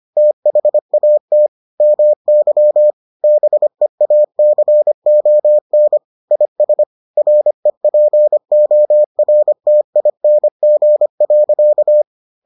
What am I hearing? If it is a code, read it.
Morse code, heard as a keyed tone, about 25 wpm, 600 Hz. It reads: THAT MY BEACON IS REPORTING.